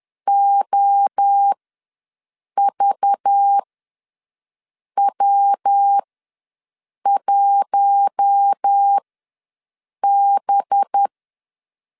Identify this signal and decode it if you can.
Morse code, heard as a keyed tone, about 10 wpm, spaced as for 6 wpm, 800 Hz. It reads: OVW1B
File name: LEZIONI / 5 LBVC / 5 LBV ws 10wpm 3645